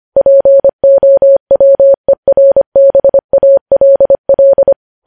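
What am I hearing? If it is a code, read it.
Morse code, heard as a keyed tone, about 25 wpm, 550 Hz. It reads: POWERBALL
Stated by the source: Morse practice of news headlines at 25wpm.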